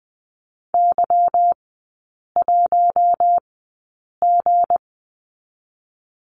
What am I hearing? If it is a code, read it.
Morse code, heard as a keyed tone, about 20 wpm, spaced as for 8 wpm, 700 Hz. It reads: Y1G
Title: 172 Lesson 28e - Number 9 - Three Characters 20wpm